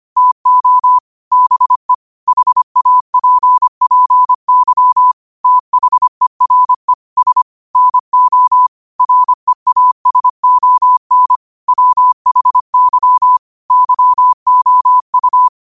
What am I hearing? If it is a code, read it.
Morse code, heard as a keyed tone, about 25 wpm, 1000 Hz. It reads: TO BE HAPPY THERE S NO REASON WHY YOU
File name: qotdfiles / 25wpm